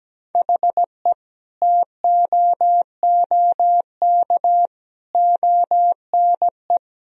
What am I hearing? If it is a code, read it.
Morse code, heard as a keyed tone, about 17 wpm, 700 Hz. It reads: HE TOOK ONE
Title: Sentences from Top 1000 Words - Review Part 2 - 17wpm